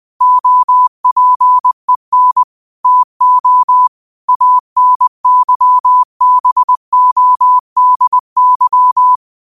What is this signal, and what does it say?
Morse code, heard as a keyed tone, about 20 wpm, 1000 Hz. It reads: OPEN TO ANYBODY
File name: qotdfiles / 20wpm